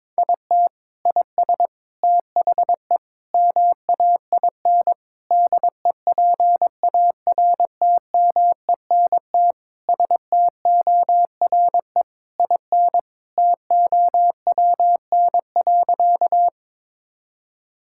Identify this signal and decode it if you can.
Morse code, heard as a keyed tone, about 21 wpm, 700 Hz. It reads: IT IS THE MAIN DEPARTMENT STORE IN TOWN.
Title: Sentences from Top 1000 Words - Review Part 5 - 22wpm